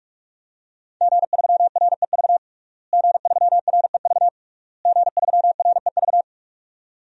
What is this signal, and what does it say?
Morse code, heard as a keyed tone, about 46 wpm, 700 Hz. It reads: G3REV G3REV G3REV
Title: DXCC-100 Call Signs - 5 Characters - Repeated 3x - 45wpm